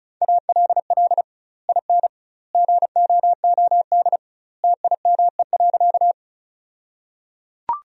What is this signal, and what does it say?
Morse code, heard as a keyed tone, about 34 wpm, 700 Hz. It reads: ALL IN GOOD TIME.
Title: 2x Word-Spacing - Sentences from Top 200 Words - 35wpm